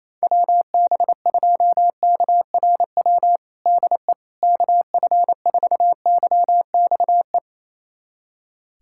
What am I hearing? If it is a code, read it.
Morse code, heard as a keyed tone, about 28 wpm, 700 Hz. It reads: WB2KRW DE KF4YXE
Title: USA Callsign DE Callsign - 6 Characters - 28wpm